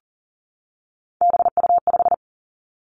Morse code, heard as a keyed tone, about 40 wpm, 700 Hz. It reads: BU5
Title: Three Challenging Letter-Number Combinations 40wpm